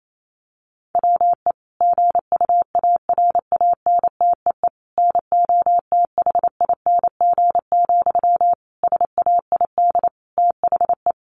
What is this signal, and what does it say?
Morse code, heard as a keyed tone, about 28 wpm, 700 Hz. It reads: WE GUARANTEE NOTHING, SAID THE
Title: A Sound Of Thunder 28wpm